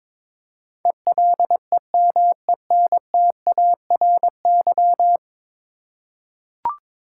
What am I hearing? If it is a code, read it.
Morse code, heard as a keyed tone, about 22 wpm, 700 Hz. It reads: ELEMENTARY
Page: morse-code-ninja-practice-22wpm